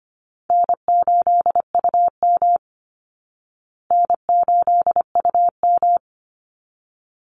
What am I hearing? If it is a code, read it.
Morse code, heard as a keyed tone, about 25 wpm, 700 Hz. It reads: N8UM N8UM
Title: Easy Call Signs Repeated 2x - 25wpm